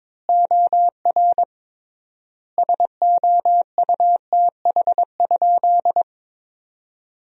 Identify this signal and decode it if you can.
Morse code, heard as a keyed tone, about 22 wpm, 700 Hz. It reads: OR SOUTH?
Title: 3x Word-Spacing - Sentences from Top 400 Words - 22wpm